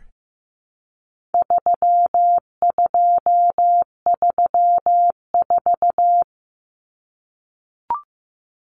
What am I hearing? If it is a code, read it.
Morse code, heard as a keyed tone, about 15 wpm, 700 Hz. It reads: 3234
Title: Common Cut Numbers - 4-Digit 15wpm